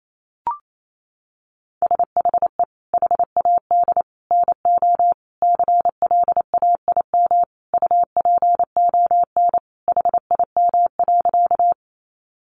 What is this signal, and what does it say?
Morse code, heard as a keyed tone, about 28 wpm, 700 Hz. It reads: SHE HAD NO CLAIM UPON HIM.
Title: Sentences from Top 1500 Words 28wpm